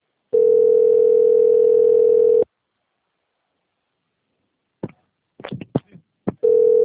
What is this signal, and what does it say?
Ringback tone (North America), 440 + 480 Hz.
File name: out- - - - - .13926